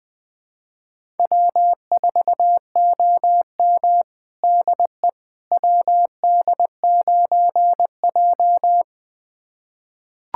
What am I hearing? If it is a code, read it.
Morse code, heard as a keyed tone, about 20 wpm, 700 Hz. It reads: W4OM DE WD9J
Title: USA Callsign DE Callsign - 4 Characters - 20wpm